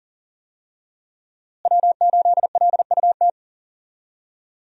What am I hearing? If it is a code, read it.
Morse code, heard as a keyed tone, about 40 wpm, 690 Hz. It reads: W8LUT
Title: USA Call Signs - 5 Characters - Repeated 3x - 40wpm